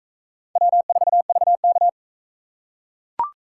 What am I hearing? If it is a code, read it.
Morse code, heard as a keyed tone, about 38 wpm, 700 Hz. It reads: W4VK